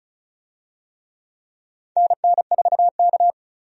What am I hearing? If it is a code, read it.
Morse code, heard as a keyed tone, about 34 wpm, 700 Hz. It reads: NN4K
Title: USA Call Signs - 4 Characters - 35wpm